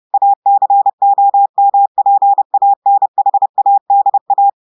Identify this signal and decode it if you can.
Morse code, heard as a keyed tone, about 30 wpm, 800 Hz. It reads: ACOMPANHADA
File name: portugais texte en clair 2 vitesse 30 mots minutes